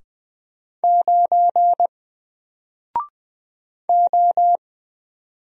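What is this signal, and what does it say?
Morse code, heard as a keyed tone, about 20 wpm, 700 Hz. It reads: 9 O
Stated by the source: Single letter and number Morse drill at 20 wpm, from N4LCD.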